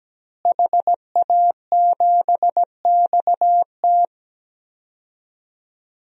Morse code, heard as a keyed tone, about 17 wpm, 700 Hz. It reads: HA7XT
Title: DXCC-100 Call Signs - 5 Characters - Repeated 2x - 17wpm